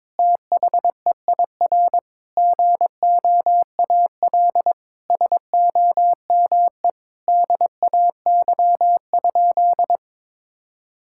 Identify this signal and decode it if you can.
Morse code, heard as a keyed tone, about 22 wpm, 700 Hz. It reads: THEIR GOAL SOME DAY?
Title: Sentences from Top 1000 Words - Review Part 6 - 22wpm